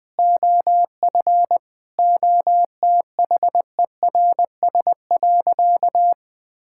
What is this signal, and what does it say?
Morse code, heard as a keyed tone, about 20 wpm, 700 Hz. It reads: OF OTHERS.
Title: Sentences with 4-Word Phrases - 20wpm